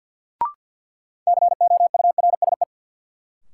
Morse code, heard as a keyed tone, about 52 wpm, 700 Hz. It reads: COURSE
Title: Sets of 2 Words - New Word Drill - 50wpm